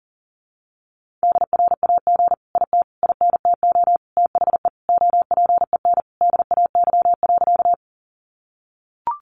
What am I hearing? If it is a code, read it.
Morse code, heard as a keyed tone, about 40 wpm, 700 Hz. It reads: DRAG IT INTO THE OPEN DAY.
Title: Sentences from Top 1500 Words 40wpm